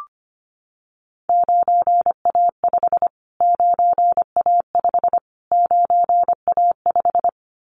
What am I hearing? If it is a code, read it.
Morse code, heard as a keyed tone, about 25 wpm, 700 Hz. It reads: 9A5 9A5 9A5
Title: Callsign Prefixes for Top 25 DXCC - Repeated 3x - 25wpm